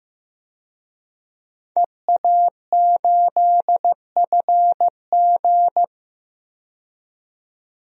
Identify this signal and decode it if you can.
Morse code, heard as a keyed tone, about 15 wpm, 700 Hz. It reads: EA8FG
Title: DXCC-100 Call Signs - 5 Characters - 15wpm